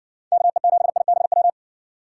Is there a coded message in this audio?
Yes: Morse code from 0.3 to 1.5 s, heard as a keyed tone, about 60 wpm, 700 Hz. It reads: KE7IBW